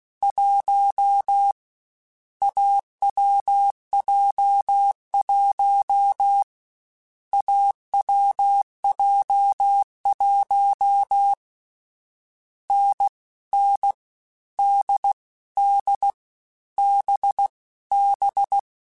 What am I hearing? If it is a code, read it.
Morse code, heard as a keyed tone, about 16 wpm, 800 Hz. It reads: 1 AWJ1 AWJ1 N N D D B B
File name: Difficult Combos / Difficult Combos-mp3s / awj1 ndb6 gz7 @15